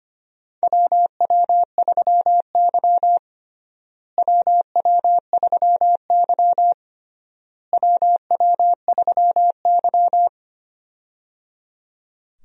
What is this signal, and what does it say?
Morse code, heard as a keyed tone, about 25 wpm, 700 Hz. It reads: WW3Y WW3Y WW3Y
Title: USA Call Signs - 4 Characters - Repeated 3x - 25wpm